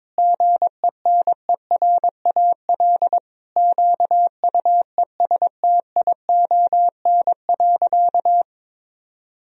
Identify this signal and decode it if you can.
Morse code, heard as a keyed tone, about 22 wpm, 700 Hz. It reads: GENERAL QUESTION.
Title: Sentences from Top 1000 Words - Review Part 3 - 22wpm